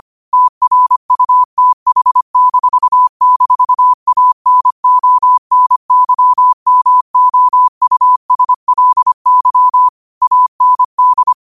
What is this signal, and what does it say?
Morse code, heard as a keyed tone, about 25 wpm, 1000 Hz. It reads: TRUTH--ANONYMOUSLY AND